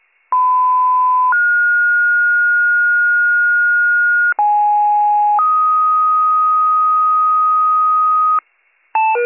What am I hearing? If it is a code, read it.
Two-tone page, heard as a one-second tone, then a three-second tone, about 1010 then 1430 Hz, about 830 then 1250 Hz, with noise. It is heard from 0.3 to 8.4 s.